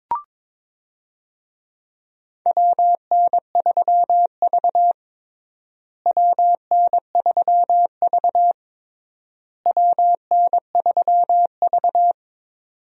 Morse code, heard as a keyed tone, about 22 wpm, 700 Hz. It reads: WN3V WN3V WN3V
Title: USA Call Signs - 4 Characters - Repeated 3x - 22wpm